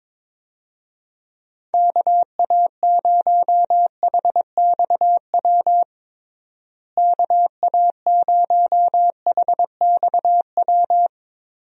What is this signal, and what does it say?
Morse code, heard as a keyed tone, about 22 wpm, 700 Hz. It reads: KA0HXW KA0HXW
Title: USA Call Signs - 6 Characters - Repeated 2x - 22wpm